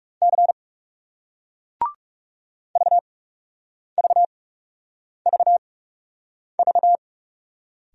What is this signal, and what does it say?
Morse code, heard as a keyed tone, about 40 wpm, spaced as for 8 wpm, 700 Hz. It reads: C VVVV